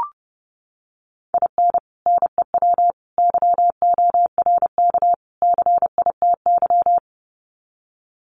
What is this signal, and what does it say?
Morse code, heard as a keyed tone, about 30 wpm, 700 Hz. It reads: IN NEW YORK CITY
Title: Sets of 4 Words 30wpm